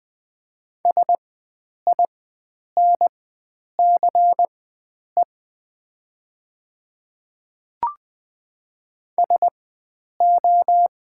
Morse code, heard as a keyed tone, about 20 wpm, spaced as for 9 wpm, 700 Hz. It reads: SINCE SO
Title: 4x Character Spacing - Top 300 Words - 20wpm